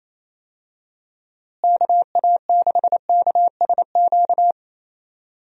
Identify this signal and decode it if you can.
Morse code, heard as a keyed tone, about 28 wpm, 700 Hz. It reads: KA6KSQ
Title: USA Call Signs - 6 Characters - Repeated 3x - 28wpm